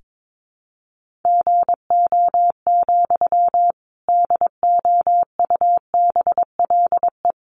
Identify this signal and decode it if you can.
Morse code, heard as a keyed tone, about 22 wpm, 700 Hz. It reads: GO, DOUBLE
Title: Sentences from Top 1400 Words 22wpm